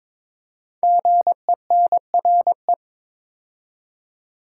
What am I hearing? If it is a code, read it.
Morse code, heard as a keyed tone, about 22 wpm, 700 Hz. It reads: GENRE
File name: 5 Letter Words - First 3 Letters Given - 22wpm